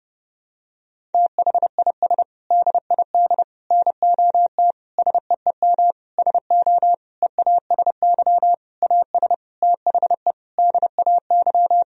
Morse code, heard as a keyed tone, about 32 wpm, 700 Hz. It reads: THIS DID NOT SEEM SO EASY AS THE DAY
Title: Sentences from Top 1000 Words - Review Part 1 - 30wpm